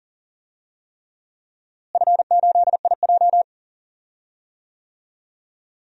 Morse code, heard as a keyed tone, about 40 wpm, 700 Hz. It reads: F8IJ
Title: DXCC-100 Call Signs - 4 Characters - Repeated 3x - 40wpm